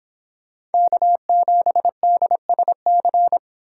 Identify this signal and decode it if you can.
Morse code, heard as a keyed tone, about 29 wpm, 700 Hz. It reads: K7DSC